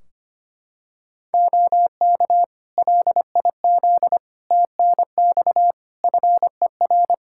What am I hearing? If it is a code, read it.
Morse code, heard as a keyed tone, about 25 wpm, 700 Hz. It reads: OK LIZ TNX FER
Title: K1USN SST QSO Practice 25wpm